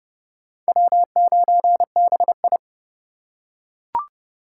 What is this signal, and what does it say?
Morse code, heard as a keyed tone, about 30 wpm, 700 Hz. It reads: W9BI
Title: USA Call Signs - 4 Characters - 3x Character Spacing - 30wpm